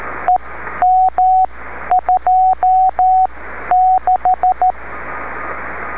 Morse code, heard as a keyed tone, about 13 wpm, spaced as for 11 wpm, 740 Hz, with radio noise. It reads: EM26
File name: beacons / w5gpm-b